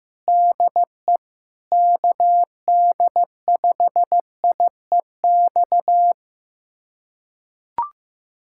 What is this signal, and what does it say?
Morse code, heard as a keyed tone, about 15 wpm, 700 Hz. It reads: DE KD5IEX
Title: USA Callsign DE Callsign 15wpm